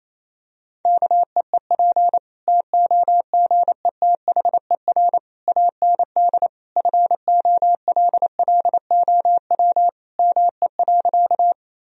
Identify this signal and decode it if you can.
Morse code, heard as a keyed tone, about 28 wpm, 700 Hz. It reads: KEEP TOGETHER AND FOLLOW ME.
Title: Sentences from Top 1000 Words - Review Part 1 - 28wpm